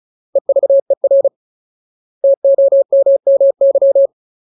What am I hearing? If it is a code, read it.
Morse code, heard as a keyed tone, about 34 wpm, 550 Hz. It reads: EVER TOMMY
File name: Ultimate Word List / 35WPM Ultimate